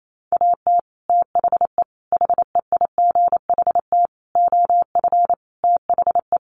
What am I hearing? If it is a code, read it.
Morse code, heard as a keyed tone, about 28 wpm, 700 Hz. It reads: AT THE HEIGHT OF THE
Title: Sets of 5 Words 28wpm